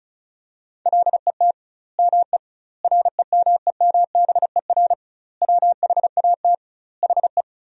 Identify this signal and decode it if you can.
Morse code, heard as a keyed tone, about 36 wpm, 700 Hz. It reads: LET ME REMEMBER WHAT HE
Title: 2x Word-Spacing - Sentences from Top 400 Words - 35wpm